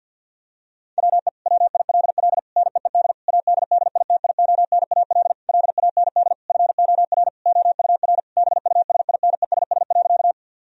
Morse code, heard as a keyed tone, about 50 wpm, 700 Hz. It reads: WE WILL NEED ADDITIONAL LAND FOR OUR BUSINESS.
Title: Sentences from Top 1000 Words - Review Part 4 - 50wpm